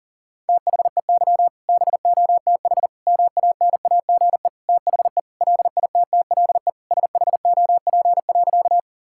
Morse code, heard as a keyed tone, about 40 wpm, 700 Hz. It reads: THEY BOTH MANAGE THE LITTLE SHOP.
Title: Sentences from Top 1600 Words 40wpm